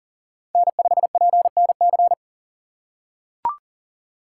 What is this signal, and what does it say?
Morse code, heard as a keyed tone, about 40 wpm, 700 Hz. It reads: N5PNC
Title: USA Call Signs - 5 Characters - 4x Character Spacing - 40wpm